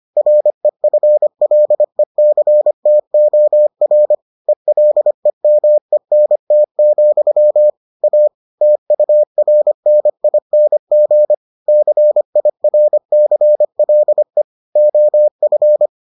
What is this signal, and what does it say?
Morse code, heard as a keyed tone, about 25 wpm, 600 Hz. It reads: REFLECTOR ELEMENT, A TURNING CIRCLE OF